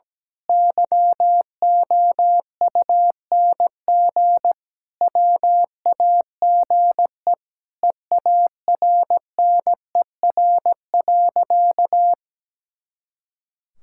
Morse code, heard as a keyed tone, about 17 wpm, 700 Hz. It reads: YOUNG WAGE EARNER.
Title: Pangram Challenge v2 - 17wpm